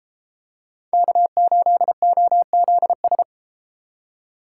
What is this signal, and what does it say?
Morse code, heard as a keyed tone, about 32 wpm, 700 Hz. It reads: K8OZS